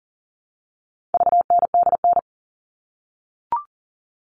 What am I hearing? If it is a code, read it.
Morse code, heard as a keyed tone, about 40 wpm, 700 Hz. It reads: VNDN
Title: All Cut Numbers - 4-Digit 40wpm